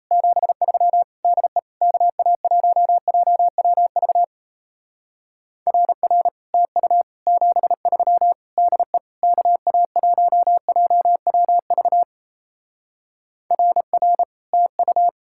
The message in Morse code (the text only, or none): 73 DE KA1JWV RR TU 73 DE KA1JWV RR TU